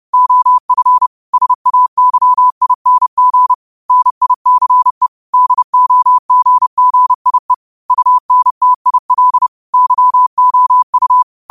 Morse code, heard as a keyed tone, about 30 wpm, 1000 Hz. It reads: OF SAYING NICE DOGGIE UNTIL YOU
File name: qotdfiles / 30wpm